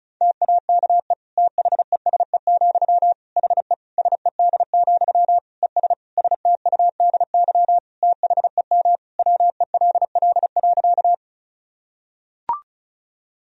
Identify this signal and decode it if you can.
Morse code, heard as a keyed tone, about 34 wpm, 700 Hz. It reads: TAKE THESE, HE SED, ES STUDY THEM WELL.
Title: Sentences from Top 300 Words and Abbreviations 35wpm